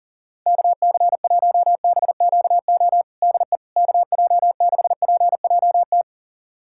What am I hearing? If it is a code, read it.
Morse code, heard as a keyed tone, about 40 wpm, 700 Hz. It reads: KC1BQO DE KJ6PJT
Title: USA Callsign DE Callsign - 6 Characters - 40wpm